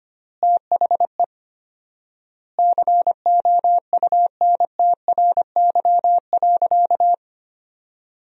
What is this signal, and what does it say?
Morse code, heard as a keyed tone, about 25 wpm, 700 Hz. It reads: THE COUNTRY.
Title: 4x Word-Spacing - Sentences from Top 300 Words - 25wpm